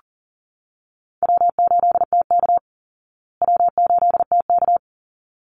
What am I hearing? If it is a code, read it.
Morse code, heard as a keyed tone, about 40 wpm, 700 Hz. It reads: W8TK W8TK
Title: Call Signs Repeated 3x - 40wpm